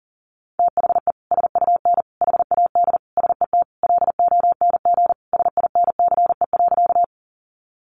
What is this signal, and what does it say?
Morse code, heard as a keyed tone, about 40 wpm, 700 Hz. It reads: THE SUN HAD SET LONG SINCE.
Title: Sentences from Top 300 Words 40wpm